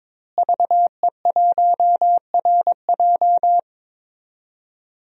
Morse code, heard as a keyed tone, about 22 wpm, 700 Hz. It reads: VE1RJ
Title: DXCC-100 Call Signs - 5 Characters - Repeated 3x - 22wpm